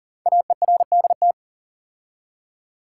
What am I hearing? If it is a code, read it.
Morse code, heard as a keyed tone, about 40 wpm, 700 Hz. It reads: AERDT
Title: Code Groups - 5 Letters - 40wpm